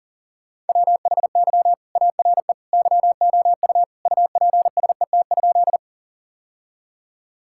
Morse code, heard as a keyed tone, about 40 wpm, 700 Hz. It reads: WHY ARE YOU UPSET?
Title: Sentences from Top 1000 Words - Review Part 5 - 40wpm